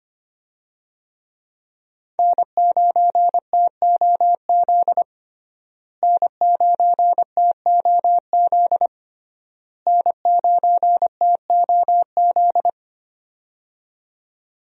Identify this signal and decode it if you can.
Morse code, heard as a keyed tone, about 25 wpm, 700 Hz. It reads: N9TOZ N9TOZ N9TOZ
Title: USA Call Signs - 5 Characters - Repeated 3x - 25wpm